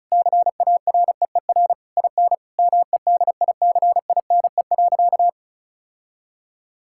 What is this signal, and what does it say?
Morse code, heard as a keyed tone, about 36 wpm, 700 Hz. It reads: CAREER IN MEDICINE.
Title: Sentences from Top 1000 Words - Review Part 6 - 35wpm